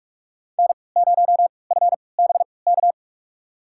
Morse code, heard as a keyed tone, about 46 wpm, spaced as for 25 wpm, 700 Hz. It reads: N0FBK
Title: USA Call Signs - 5 Characters - 3x Character Spacing - 45wpm